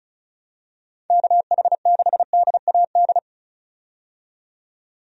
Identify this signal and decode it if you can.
Morse code, heard as a keyed tone, about 34 wpm, 700 Hz. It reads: KH6DAD
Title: DXCC-100 Call Signs - 6 Characters - 35wpm